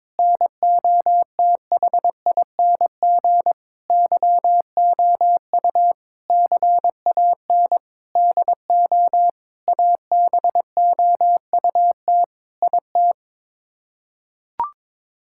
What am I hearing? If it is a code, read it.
Morse code, heard as a keyed tone, about 22 wpm, 700 Hz. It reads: NOTHING YOU CAN DO ABOUT IT
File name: practice / 22wpm / Sets of 8 Words - 22wpm